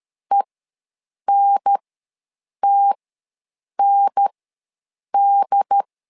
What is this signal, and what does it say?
Morse code, heard as a keyed tone, about 13 wpm, spaced as for 7 wpm, 800 Hz. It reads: ENTND
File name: LEZIONI / 2 UHDN / 2 UHDN ws 12wpm 1708